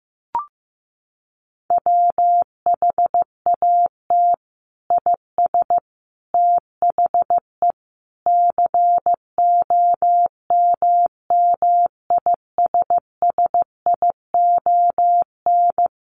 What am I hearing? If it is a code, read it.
Morse code, heard as a keyed tone, about 15 wpm, 700 Hz. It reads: WHAT IS THE COMMISSION
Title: Sentences from Top 1400 Words 15wpm